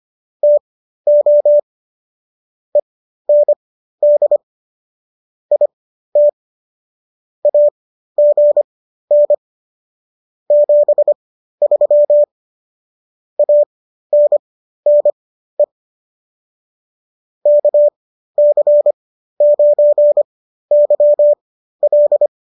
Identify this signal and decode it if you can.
Morse code, heard as a keyed tone, about 25 wpm, spaced as for 13 wpm, 590 Hz. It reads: TO END IT AGN 73 ANNE KC9YL